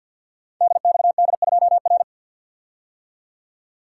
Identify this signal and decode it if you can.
Morse code, heard as a keyed tone, about 53 wpm, 700 Hz. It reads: DXD2R
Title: Code Groups - 5 Letter:Numbers - 50wpm